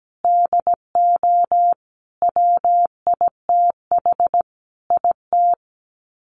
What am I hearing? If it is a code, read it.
Morse code, heard as a keyed tone, about 17 wpm, 700 Hz. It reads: DO WITH IT
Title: Sets of 5 Words Encore 17wpm